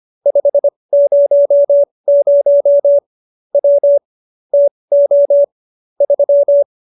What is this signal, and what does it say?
Morse code, heard as a keyed tone, about 25 wpm, spaced as for 20 wpm, 570 Hz. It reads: 500 W TO 3